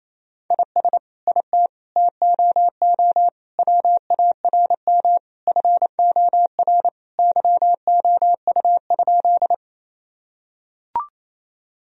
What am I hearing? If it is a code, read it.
Morse code, heard as a keyed tone, about 28 wpm, 700 Hz. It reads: IS IT TOO WARM FOR YOU?
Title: Sentences from Top 1000 Words - Review Part 2 - 28wpm